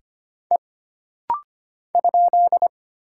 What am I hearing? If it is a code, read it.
Morse code, heard as a keyed tone, about 25 wpm, 700 Hz. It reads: E ?
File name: Single Character (Letter:Number:Punctuation:Prosign) - Rapid-Fire - 25wpm